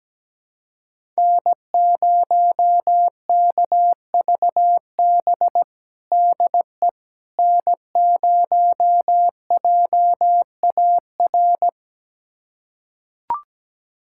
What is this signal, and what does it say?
Morse code, heard as a keyed tone, about 17 wpm, 700 Hz. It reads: N0KVB DE N0JAR
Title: USA Callsign DE Callsign - 5 Characters - 17wpm